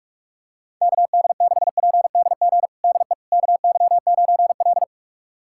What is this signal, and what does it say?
Morse code, heard as a keyed tone, about 43 wpm, 700 Hz. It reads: KD6PDG DE KY9L
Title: USA Callsign DE Callsign 45wpm